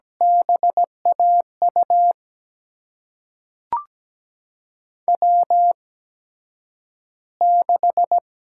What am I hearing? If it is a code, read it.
Morse code, heard as a keyed tone, about 17 wpm, spaced as for 9 wpm, 700 Hz. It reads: BAU W6
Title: USA Call Signs - 8x Character Spacing - 17wpm